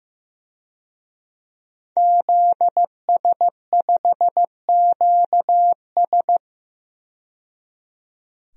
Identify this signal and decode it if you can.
Morse code, heard as a keyed tone, about 15 wpm, 700 Hz. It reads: ZS5QS
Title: DXCC-100 Call Signs - 5 Characters - 15wpm